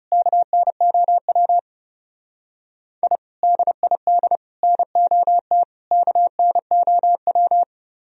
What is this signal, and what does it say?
Morse code, heard as a keyed tone, about 37 wpm, 700 Hz. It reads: KNOW I DID NOT KNOW
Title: Speed-Racing - Sets of 4 Words 30wpm